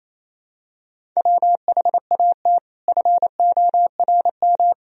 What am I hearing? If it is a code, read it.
Morse code, heard as a keyed tone, about 28 wpm, 700 Hz. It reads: WHAT FORM
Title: Sentences from Top 1000 Words - Review Part 3 - 28wpm